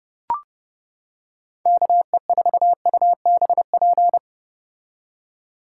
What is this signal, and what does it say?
Morse code, heard as a keyed tone, about 29 wpm, 700 Hz. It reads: KE4UBP